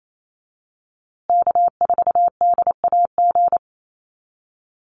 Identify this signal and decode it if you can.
Morse code, heard as a keyed tone, about 28 wpm, 700 Hz. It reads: K4DAG
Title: USA Call Signs Repeated 3x 28wpm